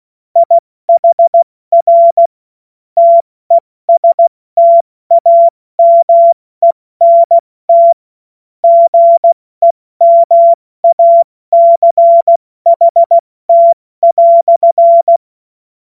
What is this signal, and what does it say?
Morse code, heard as a keyed tone, about 16 wpm, 680 Hz. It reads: IHR TESTAMENT GEMACHT"
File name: Morsekurs / Witze / Witz 26 März 2024 70 F80